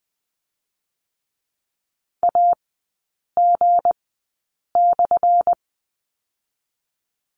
Morse code, heard as a keyed tone, about 20 wpm, spaced as for 8 wpm, 690 Hz. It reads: AG/
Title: 212 Lesson 34e - Number 8 - Three Characters 20wpm